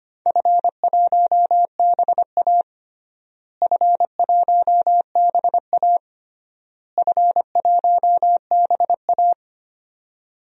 Morse code, heard as a keyed tone, about 25 wpm, 700 Hz. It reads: F1BA F1BA F1BA
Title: DXCC-100 Call Signs - 4 Characters - Repeated 3x - 25wpm